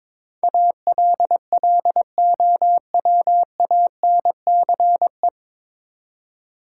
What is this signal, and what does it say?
Morse code, heard as a keyed tone, about 22 wpm, 700 Hz. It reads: ALLOWANCE
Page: Morse Code Ninja practice 22wpm